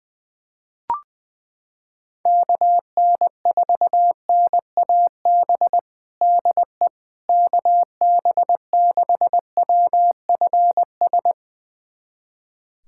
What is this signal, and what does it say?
Morse code, heard as a keyed tone, about 20 wpm, 700 Hz. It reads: KN4NAB DE KB6WFS
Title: USA Callsign DE Callsign - 6 Characters - 20wpm